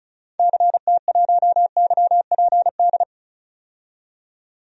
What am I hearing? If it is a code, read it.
Morse code, heard as a keyed tone, about 34 wpm, 700 Hz. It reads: CT1YPD
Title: DXCC-100 Call Signs - 6 Characters - 35wpm